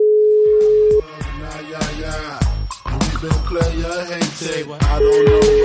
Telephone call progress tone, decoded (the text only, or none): ringback tone